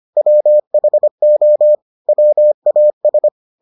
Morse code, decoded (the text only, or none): WHO WAS